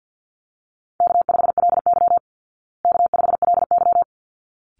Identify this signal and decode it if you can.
Morse code, heard as a keyed tone, about 50 wpm, 700 Hz. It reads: K5LY K5LY
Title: Easy Call Signs Repeated 2x - 50wpm